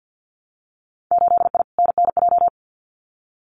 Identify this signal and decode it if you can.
Morse code, heard as a keyed tone, about 50 wpm, 700 Hz. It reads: 8I NNJ
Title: Winter Field Day Exchanges Encore 50wpm